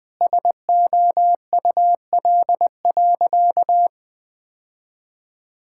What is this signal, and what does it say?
Morse code, heard as a keyed tone, about 20 wpm, 700 Hz. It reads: SOUL.
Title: Sentences with 9-Word Phrases - 20wpm